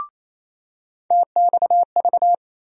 Morse code, heard as a keyed tone, about 28 wpm, 700 Hz. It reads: TXV
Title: USA Call Sign Postfixes - 3 Letters 28wpm